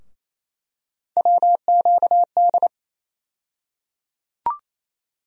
Morse code, heard as a keyed tone, about 28 wpm, 700 Hz. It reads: WQD
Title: USA Call Sign Postfixes - 3 Letters 28wpm